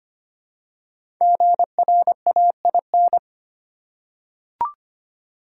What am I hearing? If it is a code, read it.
Morse code, heard as a keyed tone, about 25 wpm, 700 Hz. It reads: GRAIN
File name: 5 Letter Words - First 3 Letters Given - 25wpm